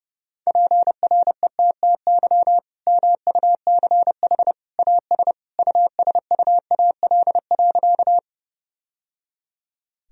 Morse code, heard as a keyed tone, about 30 wpm, 700 Hz. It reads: PRETTY MUCH AS USUAL.
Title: Sentences from Top 1000 Words - Review Part 2 - 30wpm